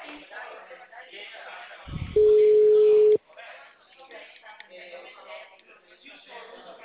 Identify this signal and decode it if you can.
Ringback tone (Europe, 425 Hz), with speech.